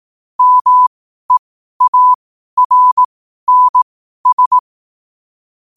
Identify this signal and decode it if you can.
Morse code, heard as a keyed tone, about 18 wpm, spaced as for 12 wpm, 1000 Hz. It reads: MEARNS
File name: qotdfiles / 12wpm